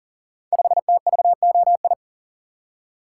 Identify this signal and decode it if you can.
Morse code, heard as a keyed tone, about 40 wpm, 690 Hz. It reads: 5TVOI
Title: Code Groups - 5 Letter:Numbers - 40wpm